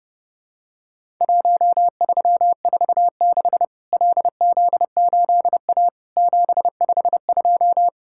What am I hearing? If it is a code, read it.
Morse code, heard as a keyed tone, about 30 wpm, 700 Hz. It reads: 1346 LZ8A 752
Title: Worked All Europe DX Contest - QTC Exchange - 30wpm